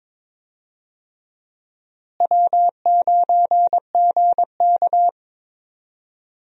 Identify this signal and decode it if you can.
Morse code, heard as a keyed tone, about 22 wpm, 700 Hz. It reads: W9GK
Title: USA Call Signs - 4 Characters - Repeated 3x - 22wpm